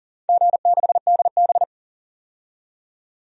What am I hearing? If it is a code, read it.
Morse code, heard as a keyed tone, about 41 wpm, 700 Hz. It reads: G6DB
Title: DXCC-100 Call Signs - 4 Characters - 40wpm